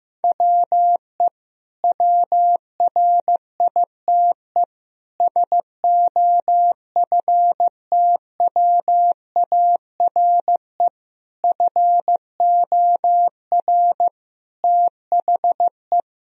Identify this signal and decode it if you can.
Morse code, heard as a keyed tone, about 15 wpm, 700 Hz. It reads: WE WRITE SOFTWARE FOR THE
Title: Sentences from Top 1000 Words - Review Part 3 - 15wpm